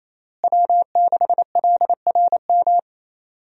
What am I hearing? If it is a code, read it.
Morse code, heard as a keyed tone, about 28 wpm, 700 Hz. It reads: W6LRM
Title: USA Call Signs - 5 Characters - Repeated 3x - 28wpm